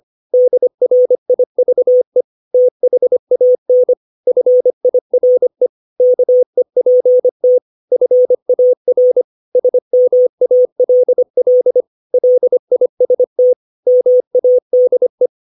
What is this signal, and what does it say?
Morse code, heard as a keyed tone, about 25 wpm, 500 Hz. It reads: DRIVE THAN FIRE KEPT FAR SMALL LIST MADE
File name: 632 words / 632 Words 25WPM